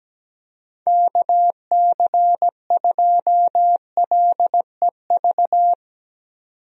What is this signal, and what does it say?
Morse code, heard as a keyed tone, about 17 wpm, 700 Hz. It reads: KC2LEV
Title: USA Call Signs - 6 Characters - 17wpm